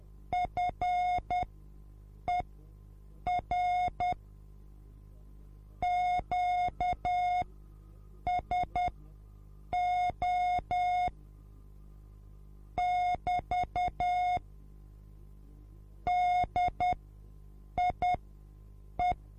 Morse code, heard as a keyed tone, about 10 wpm, spaced as for 7 wpm, 720 Hz. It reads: FER QSO <BT> DIE